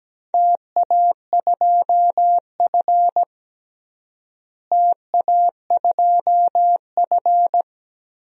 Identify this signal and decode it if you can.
Morse code, heard as a keyed tone, about 17 wpm, 700 Hz. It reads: TA2F TA2F
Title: DXCC-100 Call Signs - 4 Characters - Repeated 2x - 17wpm